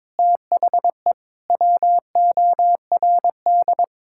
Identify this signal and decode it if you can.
Morse code, heard as a keyed tone, about 21 wpm, 700 Hz. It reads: THE WORD
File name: practice / 22wpm / Sets of 6 Words - 22wpm